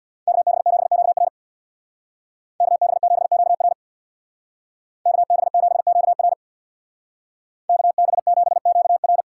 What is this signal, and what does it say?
Morse code, heard as a keyed tone, about 64 wpm, 700 Hz. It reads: KB7QR KB7QR KB7QR KB7QR